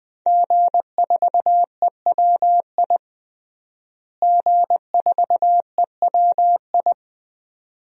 Morse code, heard as a keyed tone, about 20 wpm, 700 Hz. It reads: G4EWI G4EWI
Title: DXCC-100 Call Signs - 5 Characters - Repeated 2x - 20wpm